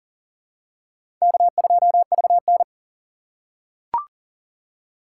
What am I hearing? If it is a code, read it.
Morse code, heard as a keyed tone, about 40 wpm, 700 Hz. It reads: K2VN
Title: USA Call Signs - 4 Characters - 2x Character Spacing - 40wpm